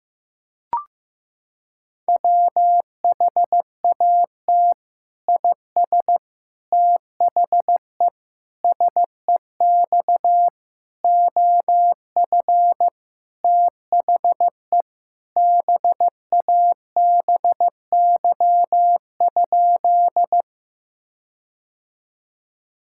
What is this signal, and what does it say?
Morse code, heard as a keyed tone, about 16 wpm, 700 Hz. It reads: WHAT IS THE SEX OF THE BABY?
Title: Sentences from Top 1600 Words 15wpm